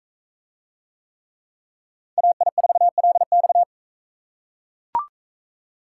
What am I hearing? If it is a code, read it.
Morse code, heard as a keyed tone, about 39 wpm, 700 Hz. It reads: AI4LX